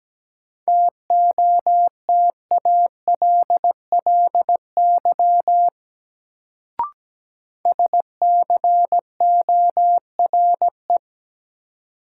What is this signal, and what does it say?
Morse code, heard as a keyed tone, about 17 wpm, 700 Hz. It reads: TOTALLY SCORE
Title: Sentences from Top 1600 Words - New Word Drill - 17wpm